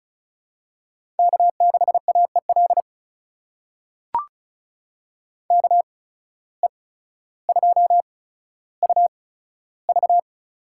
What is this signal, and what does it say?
Morse code, heard as a keyed tone, about 36 wpm, spaced as for 15 wpm, 700 Hz. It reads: K6AEL KE2UV
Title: USA Call Signs - 5 Characters - 8x Character Spacing - 35wpm